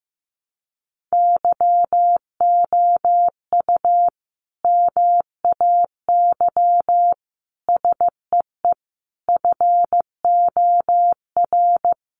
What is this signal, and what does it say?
Morse code, heard as a keyed tone, about 15 wpm, 700 Hz. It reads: YOU MAY SEE FOR
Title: Sentences from Top 600 Words 15wpm